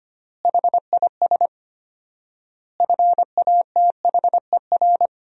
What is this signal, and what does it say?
Morse code, heard as a keyed tone, about 25 wpm, 700 Hz. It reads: HIS FATHER
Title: 4x Word-Spacing - Sentences from Top 500 Words - 25wpm